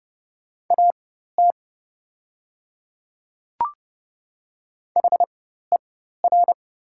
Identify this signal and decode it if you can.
Morse code, heard as a keyed tone, about 30 wpm, spaced as for 14 wpm, 690 Hz. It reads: AT HER
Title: 4x Character Spacing - Top 100 Words - 30wpm